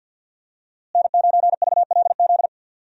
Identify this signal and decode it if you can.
Morse code, heard as a keyed tone, about 51 wpm, 700 Hz. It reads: N9VLZ
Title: USA Call Signs - 5 Characters - Repeated 2x - 50wpm